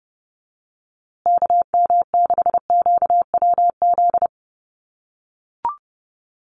Morse code, heard as a keyed tone, about 30 wpm, 700 Hz. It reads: KM6QWZ
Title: USA Call Signs Repeated 2x 30wpm